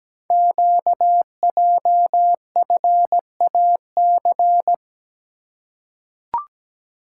Morse code, heard as a keyed tone, about 17 wpm, 700 Hz. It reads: QJFAC
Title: Code Groups - 5 Letters - 17wpm